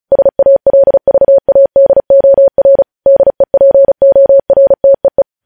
Morse code, heard as a keyed tone, about 36 wpm, 550 Hz. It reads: SALVADOR DEPORTEE
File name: podcast / News headlines-1x-35x35